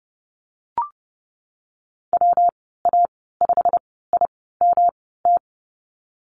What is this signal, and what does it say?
Morse code, heard as a keyed tone, about 30 wpm, 700 Hz. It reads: W A 5 I M T
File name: Lesson 19g - Letter w - Call Signs 30wpm